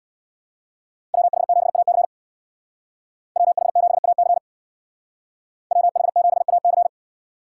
Morse code, heard as a keyed tone, about 64 wpm, 700 Hz. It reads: WH7AC WH7AC WH7AC